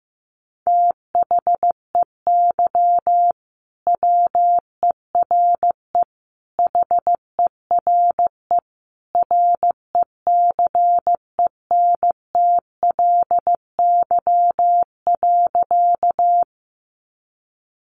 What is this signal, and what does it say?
Morse code, heard as a keyed tone, about 15 wpm, 700 Hz. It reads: THEY WERE HERE RECENTLY.
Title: Sentences from Top 1200 Words 15wpm